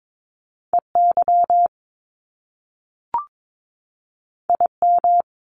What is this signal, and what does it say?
Morse code, heard as a keyed tone, about 22 wpm, 700 Hz. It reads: EY IM
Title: Top Two Letter Combinations in Words 22wpm